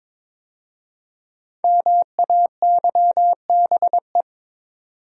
Morse code, heard as a keyed tone, about 22 wpm, 700 Hz. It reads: MAYBE
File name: 5 Letter Words - First 3 Letters Given - 22wpm